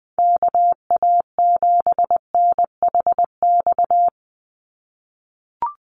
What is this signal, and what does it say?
Morse code, heard as a keyed tone, about 20 wpm, 700 Hz. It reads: KA7NHX
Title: USA Call Signs 20wpm